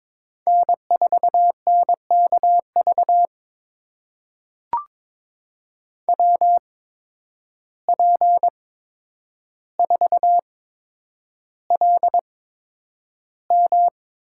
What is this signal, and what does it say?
Morse code, heard as a keyed tone, about 22 wpm, spaced as for 9 wpm, 700 Hz. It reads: N4NKV WP4LM
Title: USA Call Signs - 5 Characters - 8x Character Spacing - 22wpm